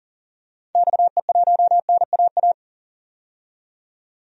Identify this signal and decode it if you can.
Morse code, heard as a keyed tone, about 38 wpm, 700 Hz. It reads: XE1NAA
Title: DXCC-100 Call Signs - 6 Characters - 40wpm